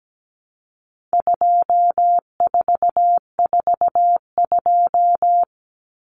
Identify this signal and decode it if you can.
Morse code, heard as a keyed tone, about 17 wpm, 700 Hz. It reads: 2442
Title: Numbers - 4-Digit 17wpm